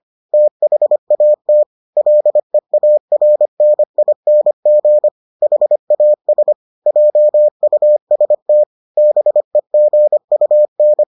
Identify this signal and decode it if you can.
Morse code, heard as a keyed tone, about 25 wpm, 600 Hz. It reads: THAT LEARNING HAS JUST BEGUN